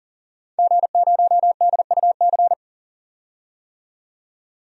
Morse code, heard as a keyed tone, about 40 wpm, 700 Hz. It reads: G0DUC
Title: DXCC-100 Call Signs - 5 Characters - 40wpm